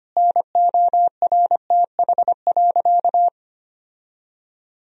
Morse code, heard as a keyed tone, about 25 wpm, 700 Hz. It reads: NORTH.
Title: Sentences from Top 1000 Words - Review Part 1 - 25wpm